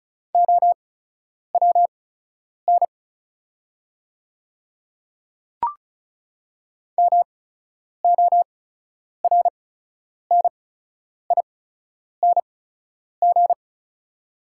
Morse code, heard as a keyed tone, about 35 wpm, spaced as for 10 wpm, 700 Hz. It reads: OWN MORNING
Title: 8x Character Spacing - Top 500 Words - 35wpm